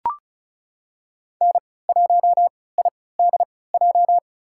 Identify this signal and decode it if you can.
Morse code, heard as a keyed tone, about 34 wpm, spaced as for 20 wpm, 690 Hz. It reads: N1IDJ
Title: USA Call Signs - 3x Character Spacing - 35wpm